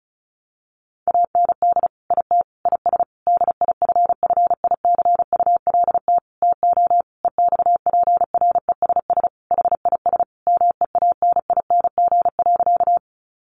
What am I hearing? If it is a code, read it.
Morse code, heard as a keyed tone, about 34 wpm, 700 Hz. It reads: AND IT IS DIFFICULT TO EXPRESS HIS MEANING.
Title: Sentences from Top 1100 Words 35wpm